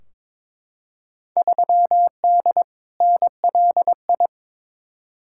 Morse code, heard as a keyed tone, about 22 wpm, 700 Hz. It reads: 3D NLI
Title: Field Day Exchanges Encore - 22wpm